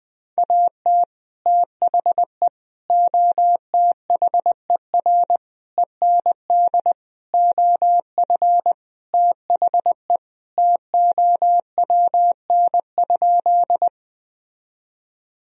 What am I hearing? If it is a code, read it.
Morse code, heard as a keyed tone, about 20 wpm, 700 Hz. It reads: AT THE OTHER END OF THE TOWN?
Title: Sentences from Top 1000 Words - Review Part 2 - 20wpm